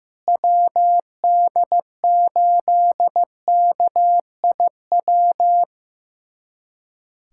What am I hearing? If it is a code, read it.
Morse code, heard as a keyed tone, about 16 wpm, 700 Hz. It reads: WD8KIW
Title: USA Callsign DE Callsign - 6 Characters - 15wpm